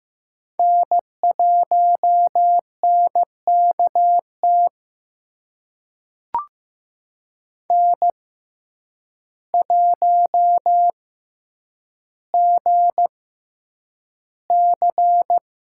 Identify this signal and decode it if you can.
Morse code, heard as a keyed tone, about 15 wpm, spaced as for 8 wpm, 700 Hz. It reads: N1NKT N1GC
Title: USA Call Signs - 5 Characters - 6x Character Spacing - 15wpm